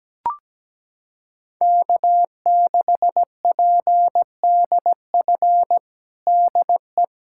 Morse code, heard as a keyed tone, about 17 wpm, 700 Hz. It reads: K6PDF DE
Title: USA Callsign DE Callsign - 5 Characters - 17wpm